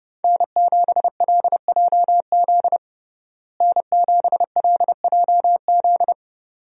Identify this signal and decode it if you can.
Morse code, heard as a keyed tone, about 30 wpm, 700 Hz. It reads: N7LJZ N7LJZ
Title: USA Call Signs - 5 Characters - Repeated 2x - 30wpm